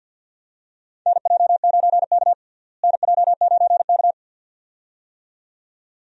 Morse code, heard as a keyed tone, about 51 wpm, 700 Hz. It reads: NJ9K NJ9K
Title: USA Call Signs - 4 Characters - Repeated 2x - 50wpm